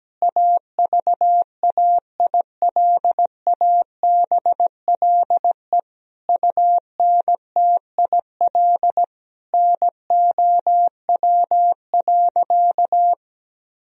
Morse code, heard as a keyed tone, about 17 wpm, 700 Hz. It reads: AVAILABLE UNTIL NOW.
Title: Sentences from Top 1000 Words - Review Part 3 - 17wpm